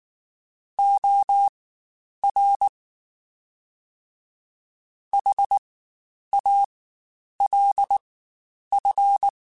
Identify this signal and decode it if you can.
Morse code, heard as a keyed tone, about 19 wpm, spaced as for 9 wpm, 800 Hz. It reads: OR HALF
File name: Spaced / Chuckles Spaced-mp3s / HalfFull sp @18